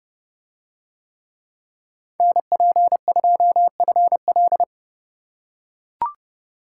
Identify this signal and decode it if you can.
Morse code, heard as a keyed tone, about 28 wpm, 700 Hz. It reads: NP2FL